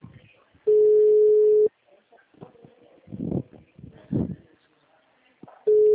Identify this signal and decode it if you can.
Ringback tone (Europe, 425 Hz), with speech.
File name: out- -2010- - - .7071